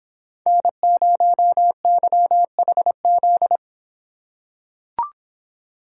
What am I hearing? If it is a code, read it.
Morse code, heard as a keyed tone, about 24 wpm, 700 Hz. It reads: N0YHZ